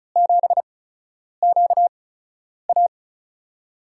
Morse code, heard as a keyed tone, about 35 wpm, spaced as for 10 wpm, 700 Hz. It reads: 7QA
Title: USA Call Signs - 4 Characters - 8x Character Spacing - 35wpm